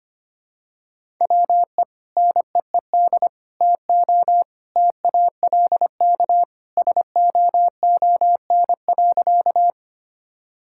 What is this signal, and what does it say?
Morse code, heard as a keyed tone, about 25 wpm, 700 Hz. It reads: WE NEED TO TALK SOON.
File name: practice / 25wpm / Sentences with 2-Word Phrases - 25wpm